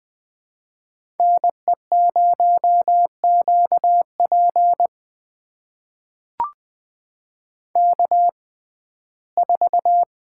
Morse code, heard as a keyed tone, about 20 wpm, spaced as for 14 wpm, 700 Hz. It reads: NE0QP K4
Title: USA Call Signs - 5 Characters - 6x Character Spacing - 20wpm